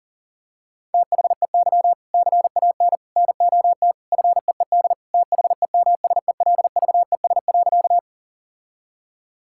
Morse code, heard as a keyed tone, about 40 wpm, 700 Hz. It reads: THEY CAN NOT FEED THEMSELVES.
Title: Sentences from Top 1600 Words 40wpm